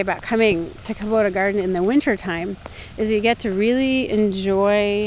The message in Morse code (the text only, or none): none